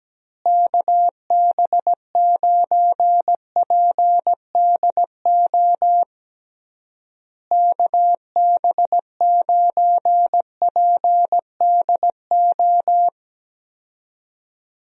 Morse code, heard as a keyed tone, about 17 wpm, 700 Hz. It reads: KB9PDO KB9PDO
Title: USA Call Signs - 6 Characters - Repeated 3x - 17wpm